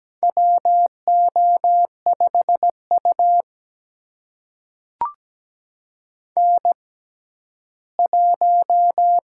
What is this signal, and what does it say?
Morse code, heard as a keyed tone, about 17 wpm, spaced as for 12 wpm, 700 Hz. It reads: WO5U N1
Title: USA Call Signs - 4 Characters - 6x Character Spacing - 17wpm